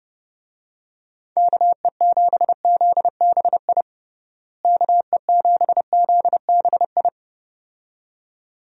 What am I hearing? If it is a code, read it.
Morse code, heard as a keyed tone, about 32 wpm, 700 Hz. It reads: KE7ZBI KE7ZBI
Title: USA Call Signs - 6 Characters - Repeated 2x - 30wpm